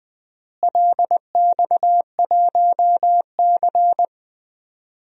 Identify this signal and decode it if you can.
Morse code, heard as a keyed tone, about 20 wpm, 700 Hz. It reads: LX1C
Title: DXCC-100 Call Signs - 4 Characters - Repeated 2x - 20wpm